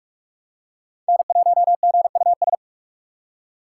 Morse code, heard as a keyed tone, about 42 wpm, 700 Hz. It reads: N1GUS